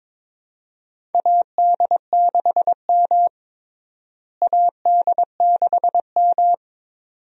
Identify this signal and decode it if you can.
Morse code, heard as a keyed tone, about 22 wpm, 700 Hz. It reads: AD6M AD6M
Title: USA Call Signs - 4 Characters - Repeated 2x - 22wpm